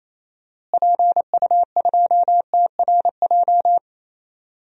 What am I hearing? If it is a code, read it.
Morse code, heard as a keyed tone, about 28 wpm, 700 Hz. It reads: PU2TRJ
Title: DXCC-100 Call Signs - 6 Characters - Repeated 3x - 28wpm